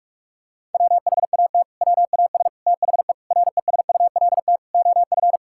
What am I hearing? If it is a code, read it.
Morse code, heard as a keyed tone, about 46 wpm, 700 Hz. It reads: WHAT WAS THE RESULT OF
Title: Sentences from Top 1000 Words - Review Part 5 - 45wpm